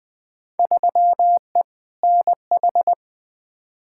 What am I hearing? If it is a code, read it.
Morse code, heard as a keyed tone, about 20 wpm, 700 Hz. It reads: 3E NH
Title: Field Day Exchanges Encore - 20wpm